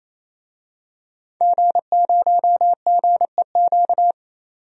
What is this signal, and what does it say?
Morse code, heard as a keyed tone, about 28 wpm, 700 Hz. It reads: G0GEQ
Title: DXCC-100 Call Signs - 5 Characters - 28wpm